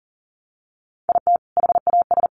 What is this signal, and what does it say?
Morse code, heard as a keyed tone, about 40 wpm, 700 Hz. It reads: IT HAS
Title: Easy Phrases 40wpm